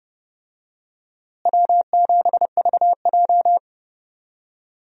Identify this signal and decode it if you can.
Morse code, heard as a keyed tone, about 29 wpm, 700 Hz. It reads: W7VJ
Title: USA Call Signs - 4 Characters - Repeated 2x - 30wpm